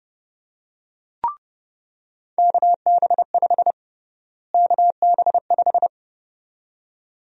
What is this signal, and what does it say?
Morse code, heard as a keyed tone, about 29 wpm, 700 Hz. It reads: KB5 KB5
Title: USA Call Sign Prefixes - Repeated 2x - 30wpm